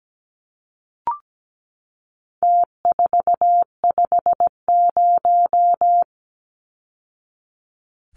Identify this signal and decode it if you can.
Morse code, heard as a keyed tone, about 17 wpm, 700 Hz. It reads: T450
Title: Numbers - Contest Style - 4-Digit 17wpm